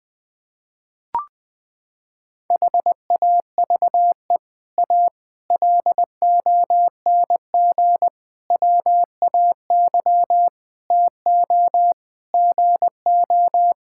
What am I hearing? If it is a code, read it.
Morse code, heard as a keyed tone, about 20 wpm, 700 Hz. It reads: HAVE A LONG WAY TO GO
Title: Sets of 6 Words - 20wpm